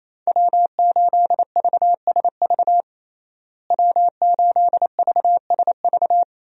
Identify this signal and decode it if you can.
Morse code, heard as a keyed tone, about 28 wpm, 700 Hz. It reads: W8VSV W8VSV
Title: USA Call Signs - 5 Characters - Repeated 2x - 28wpm